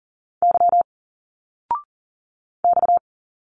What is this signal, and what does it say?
Morse code, heard as a keyed tone, about 40 wpm, 700 Hz. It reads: Y X
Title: Single Letters 40wpm